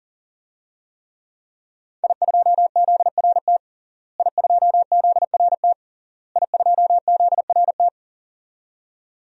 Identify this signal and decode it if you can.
Morse code, heard as a keyed tone, about 40 wpm, 700 Hz. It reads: I2ZRT I2ZRT I2ZRT
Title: DXCC-100 Call Signs - 5 Characters - Repeated 3x - 40wpm